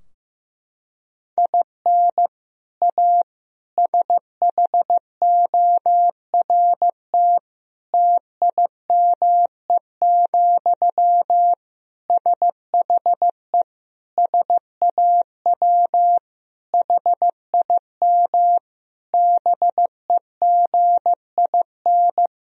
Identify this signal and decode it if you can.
Morse code, heard as a keyed tone, about 16 wpm, 700 Hz. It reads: IN A SHORT TIME, SHE SAW HIM BEGIN
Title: Sentences from Top 1000 Words - Review Part 2 - 15wpm